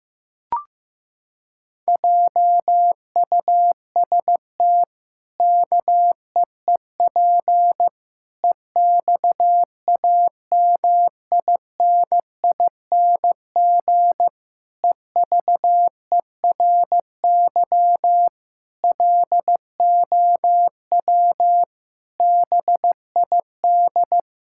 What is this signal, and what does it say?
Morse code, heard as a keyed tone, about 15 wpm, 700 Hz. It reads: JUST KEEP EXAMINING EVERY LOW BID
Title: Pangram Challenge v2 - 15wpm